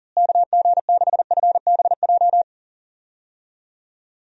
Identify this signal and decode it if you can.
Morse code, heard as a keyed tone, about 40 wpm, 700 Hz. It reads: KG6FBJ
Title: USA Call Signs - 6 Characters - Repeated 2x - 40wpm